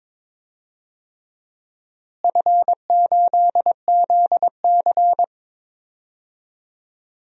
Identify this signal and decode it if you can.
Morse code, heard as a keyed tone, about 22 wpm, 700 Hz. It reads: F8ZC
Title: DXCC-100 Call Signs - 4 Characters - Repeated 3x - 22wpm